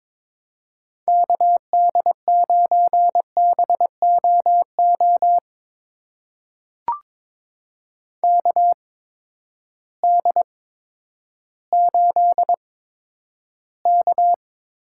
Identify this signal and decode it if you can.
Morse code, heard as a keyed tone, about 22 wpm, spaced as for 11 wpm, 700 Hz. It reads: KD9BOO KD8K
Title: USA Call Signs - 6 Characters - 8x Character Spacing - 22wpm